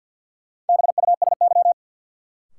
Morse code, heard as a keyed tone, about 53 wpm, 700 Hz. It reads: BUSY
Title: Sentences from Top 1000 Words - Review Part 2 - New Word Drill - 50wpm